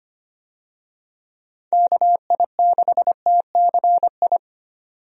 Morse code, heard as a keyed tone, about 25 wpm, 700 Hz. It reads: KI6TCI
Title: USA Call Signs - 6 Characters - 25wpm